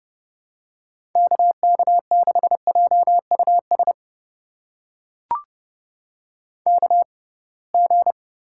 Morse code, heard as a keyed tone, about 30 wpm, spaced as for 23 wpm, 700 Hz. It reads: KK6JUS KG
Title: USA Call Signs - 6x Character Spacing - 30wpm